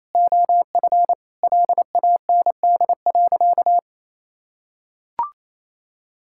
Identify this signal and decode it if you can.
Morse code, heard as a keyed tone, about 28 wpm, 700 Hz. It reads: OF LAND.
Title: Sentences from Top 1000 Words - Review Part 6 - 28wpm